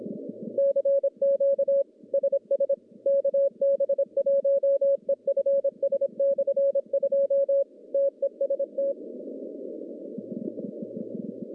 Morse code, heard as a keyed tone, about 26 wpm, 550 Hz. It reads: CQ SS KB1EFS/2 TEST
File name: contests / CW-SS-k5tr- -1100